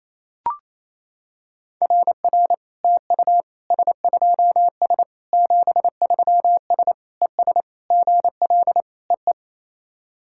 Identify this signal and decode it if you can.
Morse code, heard as a keyed tone, about 28 wpm, 700 Hz. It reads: RR TU S2S 73S ES GL EE
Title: SOTA QSO Practice 28wpm